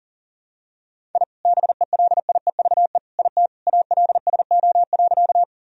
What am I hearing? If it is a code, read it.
Morse code, heard as a keyed tone, about 37 wpm, 700 Hz. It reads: I BELIEVE IT ALSO.
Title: Sentences from Top 1000 Words - Review Part 1 - 40wpm